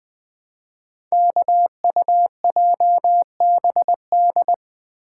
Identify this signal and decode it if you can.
Morse code, heard as a keyed tone, about 20 wpm, 700 Hz. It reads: KUJBD
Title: Code Groups - 5 Letter:Numbers - 20wpm